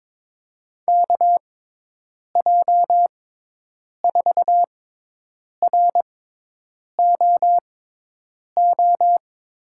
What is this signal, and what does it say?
Morse code, heard as a keyed tone, about 22 wpm, spaced as for 8 wpm, 700 Hz. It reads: KJ4ROO
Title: USA Call Signs - 6 Characters - 6x Character Spacing - 22wpm